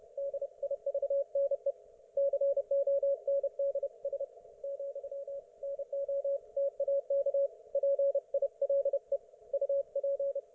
Morse code, heard as a keyed tone, about 29 wpm, 560 Hz. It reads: DIVNE CONDS , NO TAK PILE UP